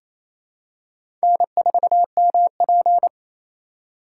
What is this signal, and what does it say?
Morse code, heard as a keyed tone, about 28 wpm, 700 Hz. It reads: N4MP
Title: USA Call Signs - 4 Characters - 2x Character Spacing - 28wpm